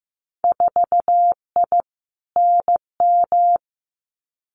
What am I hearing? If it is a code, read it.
Morse code, heard as a keyed tone, about 15 wpm, 700 Hz. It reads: 4I NM
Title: Winter Field Day Exchanges Encore 15wpm